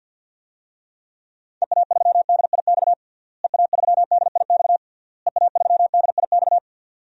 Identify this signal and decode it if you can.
Morse code, heard as a keyed tone, about 53 wpm, 700 Hz. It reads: EA3DIX EA3DIX EA3DIX
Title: DXCC-100 Call Signs - 6 Characters - Repeated 3x - 50wpm